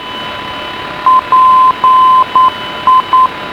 Morse code, heard as a keyed tone, about 9 wpm, 1030 Hz, with radio noise. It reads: PI